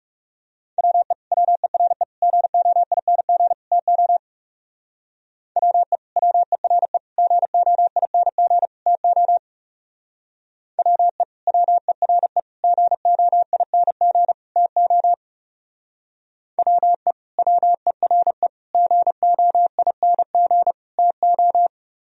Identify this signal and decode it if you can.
Morse code, heard as a keyed tone, about 36 wpm, 700 Hz. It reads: WE WERE GOING TO WE WERE GOING TO WE WERE GOING TO WE WERE GOING TO
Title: Speed-Racing - Sets of 4 Words 30wpm